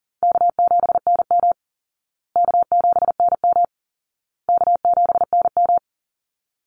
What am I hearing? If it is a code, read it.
Morse code, heard as a keyed tone, about 40 wpm, 700 Hz. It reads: K7NM K7NM K7NM
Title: Call Signs Repeated 2x - 40wpm